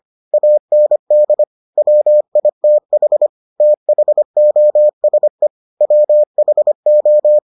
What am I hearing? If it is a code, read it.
Morse code, heard as a keyed tone, about 25 wpm, 600 Hz. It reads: AND WITH THOSE WHO